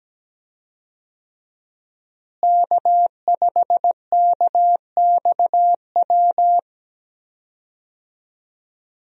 Morse code, heard as a keyed tone, about 17 wpm, 700 Hz. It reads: K5KXW
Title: USA Call Signs - 5 Characters - Repeated 3x - 17wpm